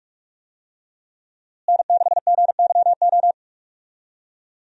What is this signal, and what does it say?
Morse code, heard as a keyed tone, about 46 wpm, 700 Hz. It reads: N6GYO
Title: USA Call Signs - 5 Characters - Repeated 3x - 45wpm